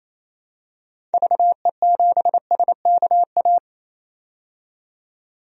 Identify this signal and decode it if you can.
Morse code, heard as a keyed tone, about 28 wpm, 700 Hz. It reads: VE7SKA
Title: DXCC-100 Call Signs - 6 Characters - 28wpm